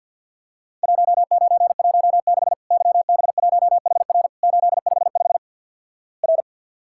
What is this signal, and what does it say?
Morse code, heard as a keyed tone, about 48 wpm, 700 Hz. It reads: 1916 YB1HR 855 R
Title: Worked All Europe DX Contest - QTC Exchange with Repeats - 50wpm